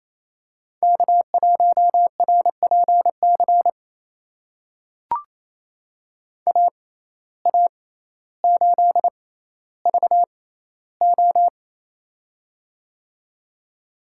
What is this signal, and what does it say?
Morse code, heard as a keyed tone, about 28 wpm, spaced as for 14 wpm, 700 Hz. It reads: K1RPC AA8VO
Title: USA Call Signs - 5 Characters - 6x Character Spacing - 28wpm